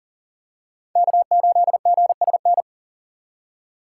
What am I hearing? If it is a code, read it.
Morse code, heard as a keyed tone, about 40 wpm, 700 Hz. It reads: K8GSN
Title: USA Call Signs - 5 Characters - 3x Character Spacing - 40wpm